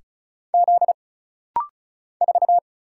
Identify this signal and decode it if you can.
Morse code, heard as a keyed tone, about 34 wpm, 700 Hz. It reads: Z 4
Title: Single Letter-Number Alt - Rapid-Fire 35wpm